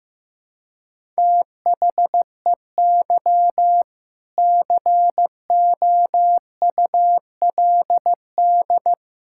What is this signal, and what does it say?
Morse code, heard as a keyed tone, about 15 wpm, 700 Hz. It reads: THEY COULD
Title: Sentences from Top 1000 Words - Review Part 6 - 15wpm